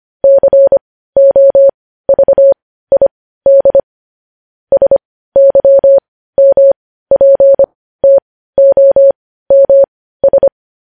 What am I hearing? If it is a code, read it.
Morse code, heard as a keyed tone, about 25 wpm, spaced as for 15 wpm, 550 Hz. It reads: COVID SYMPTOMS
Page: News Headlines in Morse Code at 15 WPM